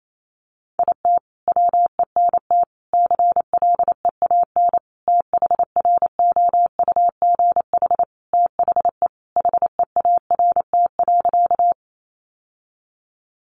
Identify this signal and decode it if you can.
Morse code, heard as a keyed tone, about 28 wpm, 700 Hz. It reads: IT WENT CLEAN THROUGH THE HEART.
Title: Sentences from Top 1200 Words 28wpm